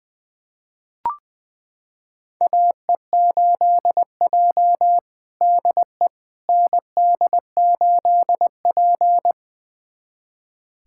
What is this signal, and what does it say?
Morse code, heard as a keyed tone, about 20 wpm, 700 Hz. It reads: AE8J DE ND8P
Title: USA Callsign DE Callsign - 4 Characters - 20wpm